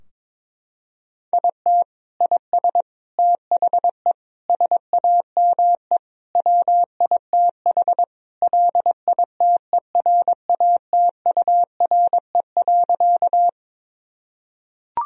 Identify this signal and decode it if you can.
Morse code, heard as a keyed tone, about 21 wpm, 700 Hz. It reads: IT IS THE SAME WITH LITERATURE.
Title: Sentences from Top 1000 Words - Review Part 3 - 22wpm